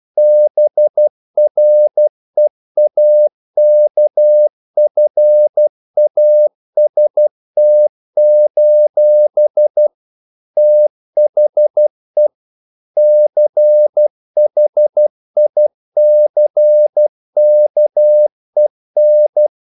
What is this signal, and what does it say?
Morse code, heard as a keyed tone, about 12 wpm, 600 Hz. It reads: BREAKFAST: THE CHICKEN
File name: CW / Practice / EnglishQuotations0112